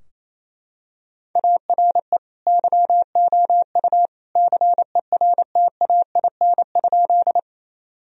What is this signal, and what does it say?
Morse code, heard as a keyed tone, about 28 wpm, 700 Hz. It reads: ARE YOU CERTAIN?
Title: Sentences from Top 1000 Words - Review Part 2 - 28wpm